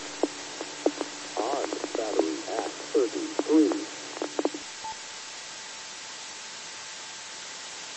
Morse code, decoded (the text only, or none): none